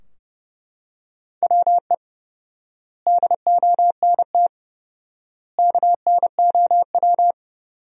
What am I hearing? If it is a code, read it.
Morse code, heard as a keyed tone, about 29 wpm, 700 Hz. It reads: WE DONT KNOW
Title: 4x Word-Spacing - Sentences from Top 400 Words - 30wpm